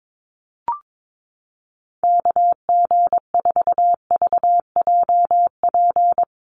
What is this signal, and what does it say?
Morse code, heard as a keyed tone, about 22 wpm, 700 Hz. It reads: KG4VJP
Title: USA Call Signs 22wpm